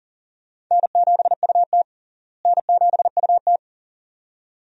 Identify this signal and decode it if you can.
Morse code, heard as a keyed tone, about 40 wpm, 700 Hz. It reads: N7UT N7UT
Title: USA Call Signs - 4 Characters - Repeated 2x - 40wpm